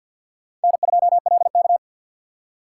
Morse code, heard as a keyed tone, about 51 wpm, 700 Hz. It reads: N2LK
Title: K1USN SST QSO Practice - 50wpm